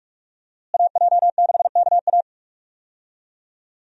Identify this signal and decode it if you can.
Morse code, heard as a keyed tone, about 46 wpm, 700 Hz. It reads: AJ6KA
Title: USA Call Signs - 5 Characters - Repeated 2x - 45wpm